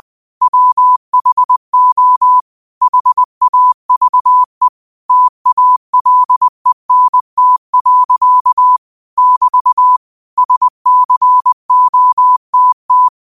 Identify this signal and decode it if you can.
Morse code, heard as a keyed tone, about 20 wpm, 1000 Hz. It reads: WHO HAVE TALENT. <BT> SCOTT